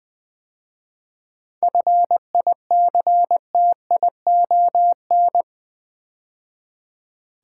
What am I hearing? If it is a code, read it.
Morse code, heard as a keyed tone, about 20 wpm, 700 Hz. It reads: FICTION
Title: 7 Letter Words - First 3 Letters Given - 20wpm